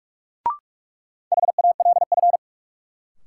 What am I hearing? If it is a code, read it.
Morse code, heard as a keyed tone, about 46 wpm, 700 Hz. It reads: HALF
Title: Sets of 3 Words - New Word Drill - 45wpm